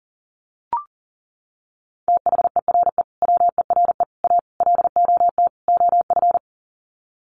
Morse code, heard as a keyed tone, about 37 wpm, 700 Hz. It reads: THERE WERE A LOT OF
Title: Sets of 5 Words 40wpm